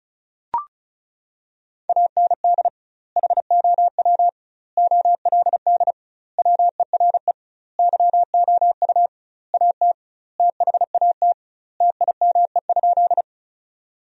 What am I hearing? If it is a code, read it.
Morse code, heard as a keyed tone, about 36 wpm, 700 Hz. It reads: AND HOW OLD WERE YOU AT THAT TIME?
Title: 2x Word-Spacing - Sentences from Top 200 Words - 35wpm